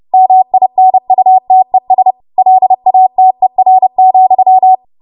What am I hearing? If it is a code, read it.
Morse code, heard as a keyed tone, about 30 wpm, 750 Hz. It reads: MINUTES LATER,
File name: Morse / 30WPM